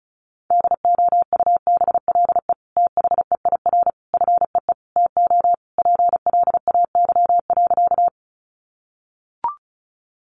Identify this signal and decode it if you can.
Morse code, heard as a keyed tone, about 34 wpm, 700 Hz. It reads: DOUBLE THEIR FEE TO PLAY.
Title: Sentences from Top 1500 Words 35wpm